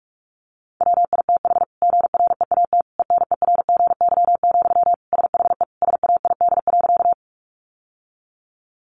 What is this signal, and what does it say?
Morse code, heard as a keyed tone, about 44 wpm, 700 Hz. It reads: WITH GREAT ENERGY, SHE SAID.
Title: Sentences from Top 900 Words 45wpm